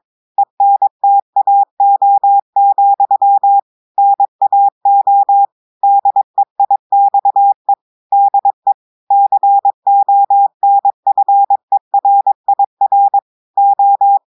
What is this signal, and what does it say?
Morse code, heard as a keyed tone, about 22 wpm, 800 Hz. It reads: ENTAO, NAO DEIXE DE CONFERIR O
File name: audios / portugais / portugais texte en clair 2 vitesse 22 mots minutes